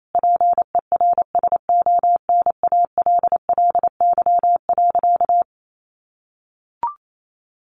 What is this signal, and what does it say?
Morse code, heard as a keyed tone, about 28 wpm, 700 Hz. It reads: PERSONALLY.
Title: Sentences from Top 1400 Words 28wpm